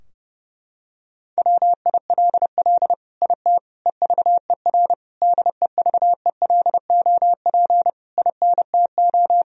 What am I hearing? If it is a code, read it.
Morse code, heard as a keyed tone, about 30 wpm, 700 Hz. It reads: WILL IT EVER DEVELOP INTO
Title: Sentences from Top 1000 Words - Review Part 2 - 30wpm